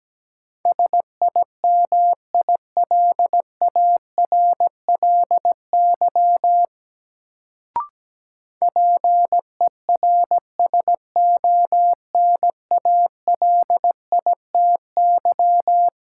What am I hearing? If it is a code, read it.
Morse code, heard as a keyed tone, about 17 wpm, 700 Hz. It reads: SIMILARLY PERSONALITY
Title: Sentences from Top 1400 Words - New Word Drill - 17wpm